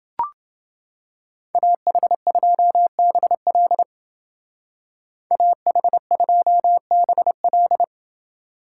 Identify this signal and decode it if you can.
Morse code, heard as a keyed tone, about 29 wpm, 700 Hz. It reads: AH2BL AH2BL